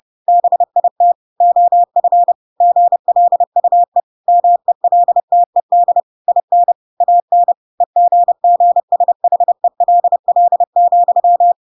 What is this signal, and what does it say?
Morse code, heard as a keyed tone, about 30 wpm, 700 Hz. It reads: BIT OF GLUE MELTED IN AN EGGSHELL,
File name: Pinocchio / Pinocchio ch8 30 30 ik1hge